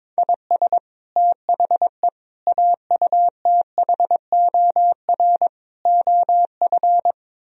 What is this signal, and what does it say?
Morse code, heard as a keyed tone, about 23 wpm, 700 Hz. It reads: IS THE AUTHOR OF
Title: Speed-Racing - Sets of 4 Words 22wpm